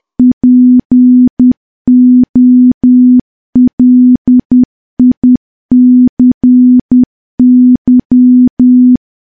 Morse code, heard as a keyed tone, about 10 wpm, 250 Hz. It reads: POLICY
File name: morse / morse-training-10WPM